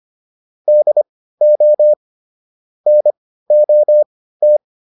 Morse code, heard as a keyed tone, about 25 wpm, spaced as for 15 wpm, 600 Hz. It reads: DO NOT
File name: WD305 15